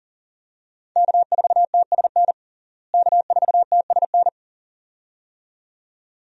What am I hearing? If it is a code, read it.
Morse code, heard as a keyed tone, about 40 wpm, 700 Hz. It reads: K4TSN K4TSN
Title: USA Call Signs - 5 Characters - Repeated 2x - 40wpm